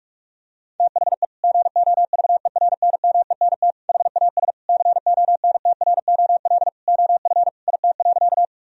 Morse code, heard as a keyed tone, about 42 wpm, 700 Hz. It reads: THE GOVERNMENT HAS CONTROL OF IT.
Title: Sentences from Top 1000 Words - Review Part 3 - 45wpm